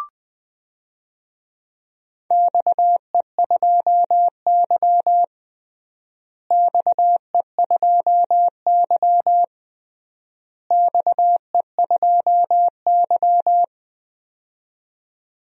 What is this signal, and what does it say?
Morse code, heard as a keyed tone, about 20 wpm, 700 Hz. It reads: XE2Y XE2Y XE2Y
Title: DXCC-100 Call Signs - 4 Characters - Repeated 3x - 20wpm